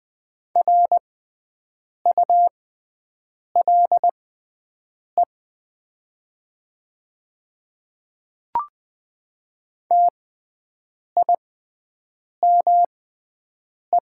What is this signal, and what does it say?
Morse code, heard as a keyed tone, about 20 wpm, spaced as for 7 wpm, 700 Hz. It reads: RULE TIME
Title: 6x Character Spacing - Top 500 Words - 20wpm